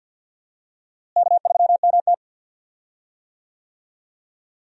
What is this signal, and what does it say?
Morse code, heard as a keyed tone, about 46 wpm, 700 Hz. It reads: K3MT
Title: USA Call Signs - 4 Characters - Repeated 2x - 50wpm